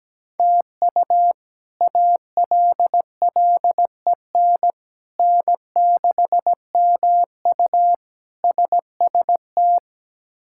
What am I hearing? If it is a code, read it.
Morse code, heard as a keyed tone, about 17 wpm, 700 Hz. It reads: TU ALLEN N6MU SST
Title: K1USN SST QSO Practice - 17wpm